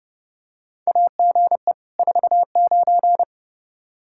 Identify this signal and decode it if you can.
Morse code, heard as a keyed tone, about 30 wpm, 700 Hz. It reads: AGE 49
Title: Practice QSO - Exchange 3 - More Info - 30wpm